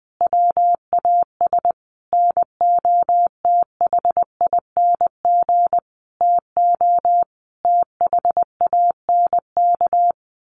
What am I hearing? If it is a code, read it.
Morse code, heard as a keyed tone, about 20 wpm, 700 Hz. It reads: WAS NOTHING TO THANK
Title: Sentences from Top 1000 Words 20wpm